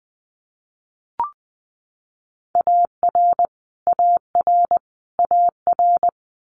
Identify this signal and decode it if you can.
Morse code, heard as a keyed tone, about 20 wpm, 700 Hz. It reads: AR AR AR
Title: US States Repeated 3x 20wpm